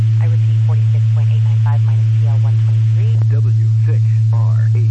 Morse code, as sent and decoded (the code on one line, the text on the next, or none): none